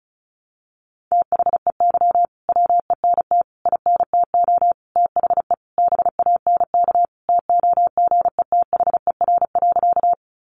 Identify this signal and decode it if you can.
Morse code, heard as a keyed tone, about 34 wpm, 700 Hz. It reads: THEY WENT INTO THE BANK TOGETHER.
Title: Sentences from Top 900 Words 35wpm